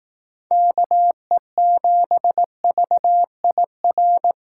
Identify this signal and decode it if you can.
Morse code, heard as a keyed tone, about 18 wpm, 700 Hz. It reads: KE7VIR
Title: Speed Racing - USA 2x3 Call Signs 15wpm